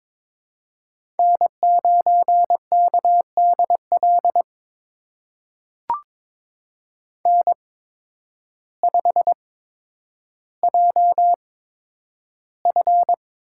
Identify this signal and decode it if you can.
Morse code, heard as a keyed tone, about 22 wpm, spaced as for 10 wpm, 700 Hz. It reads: N9KDL N5JF
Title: USA Call Signs - 5 Characters - 8x Character Spacing - 22wpm